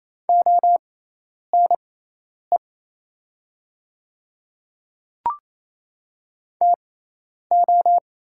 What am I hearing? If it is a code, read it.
Morse code, heard as a keyed tone, about 28 wpm, spaced as for 10 wpm, 700 Hz. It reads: ONE TO
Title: 6x Character Spacing - Top 100 Words - 28wpm